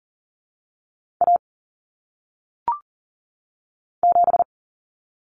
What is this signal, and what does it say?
Morse code, heard as a keyed tone, about 40 wpm, 700 Hz. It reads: A 7
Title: Common Cut Numbers - 1-Digit 40wpm